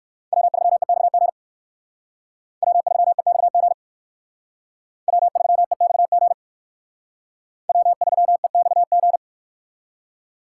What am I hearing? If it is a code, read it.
Morse code, heard as a keyed tone, about 59 wpm, 700 Hz. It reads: W3EXG W3EXG W3EXG W3EXG